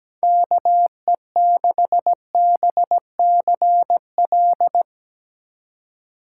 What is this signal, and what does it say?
Morse code, heard as a keyed tone, about 17 wpm, 700 Hz. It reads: KE6BCL
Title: USA Call Signs - 3x Character Spacing - 17wpm